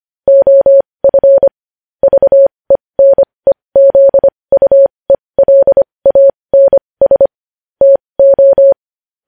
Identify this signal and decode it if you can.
Morse code, heard as a keyed tone, about 25 wpm, spaced as for 20 wpm, 550 Hz. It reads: OF VENEZUELANS TO
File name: podcast / News headlines-1x-25x20